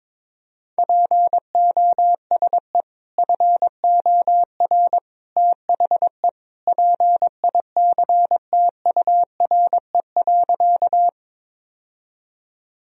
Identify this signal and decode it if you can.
Morse code, heard as a keyed tone, about 22 wpm, 700 Hz. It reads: POSE FOR THE PICTURE.
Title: Sentences from Top 1000 Words - Review Part 2 - 22wpm